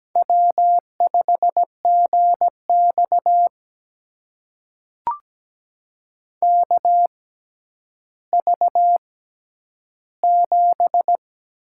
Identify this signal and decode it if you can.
Morse code, heard as a keyed tone, about 17 wpm, spaced as for 10 wpm, 700 Hz. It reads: W5GX KV7
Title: USA Call Signs - 4 Characters - 6x Character Spacing - 17wpm